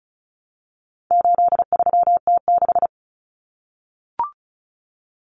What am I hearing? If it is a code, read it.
Morse code, heard as a keyed tone, about 34 wpm, 700 Hz. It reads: 83T6
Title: Common Cut Numbers - 4-Digit 35wpm